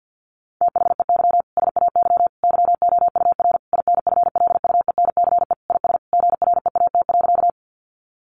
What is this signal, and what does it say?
Morse code, heard as a keyed tone, about 50 wpm, 700 Hz. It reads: THEY SAY YOUR INFLUENCE IS GREAT.
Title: Sentences from Top 1100 Words 50wpm